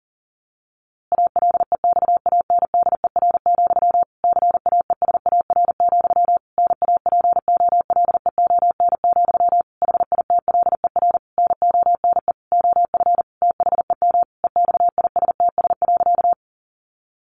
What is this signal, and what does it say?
Morse code, heard as a keyed tone, about 37 wpm, 700 Hz. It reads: ALEXANDER, CAESAR, NAPOLEON, HITLER NONE OF THEM EXISTS.
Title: A Sound Of Thunder 40wpm